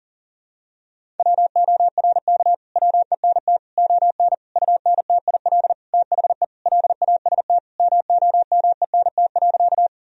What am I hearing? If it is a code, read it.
Morse code, heard as a keyed tone, about 37 wpm, 700 Hz. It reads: WORK WENT ON UNTIL THE LAST MOMENT.
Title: Sentences from Top 1000 Words - Review Part 1 - 40wpm